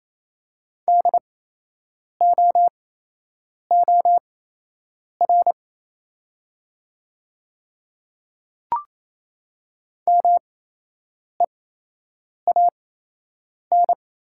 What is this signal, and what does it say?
Morse code, heard as a keyed tone, about 27 wpm, spaced as for 8 wpm, 700 Hz. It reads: DOOR MEAN
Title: 8x Character Spacing - Top 300 Words - 28wpm